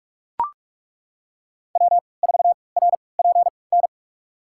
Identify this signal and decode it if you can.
Morse code, heard as a keyed tone, about 42 wpm, spaced as for 25 wpm, 700 Hz. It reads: W4RPN
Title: USA Call Signs - 5 Characters - 3x Character Spacing - 45wpm